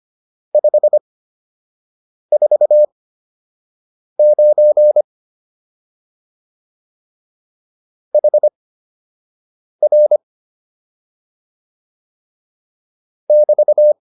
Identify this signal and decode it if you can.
Morse code, heard as a keyed tone, about 25 wpm, spaced as for 6 wpm, 600 Hz. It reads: 549 HR <BT>
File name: QSO 204 06